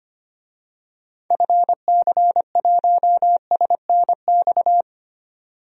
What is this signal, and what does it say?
Morse code, heard as a keyed tone, about 25 wpm, 700 Hz. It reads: FC1SNX
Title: DXCC-100 Call Signs - 6 Characters - 25wpm